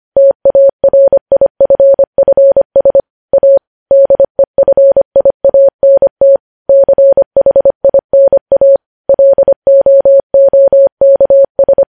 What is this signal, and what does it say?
Morse code, heard as a keyed tone, about 25 wpm, 550 Hz. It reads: TARIFFS A DEFIANT CHINA LOOKS
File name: podcast / News headlines-1x-25x25